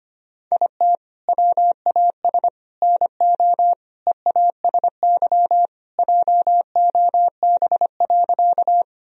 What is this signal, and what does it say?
Morse code, heard as a keyed tone, about 25 wpm, 700 Hz. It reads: IT WAS NO EASY JOB.
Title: Sentences from Top 1000 Words - Review Part 3 - 25wpm